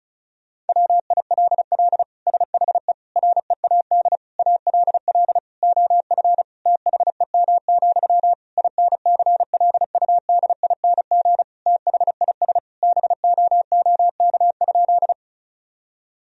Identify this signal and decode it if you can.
Morse code, heard as a keyed tone, about 34 wpm, 700 Hz. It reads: WILL SHE READ ALL OF THEM, INCLUDING THIS BOOK?
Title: Sentences from Top 1000 Words - Review Part 4 - 35wpm